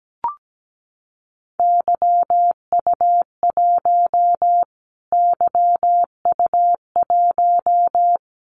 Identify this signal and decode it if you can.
Morse code, heard as a keyed tone, about 17 wpm, 700 Hz. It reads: YU1 YU1
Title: Callsign Prefixes for Top 25 DXCC - Repeated 3x - 17wpm